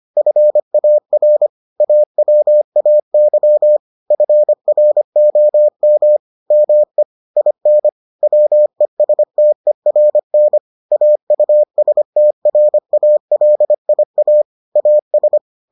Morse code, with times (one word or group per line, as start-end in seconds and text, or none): 0.2-1.5 FAR
1.8-3.8 AWAY
4.1-6.2 FROM
6.5-7.0 ME
7.4-7.9 IN
8.2-10.6 WESTERN
10.9-14.4 AUSTRALIA
14.8-15.4 AS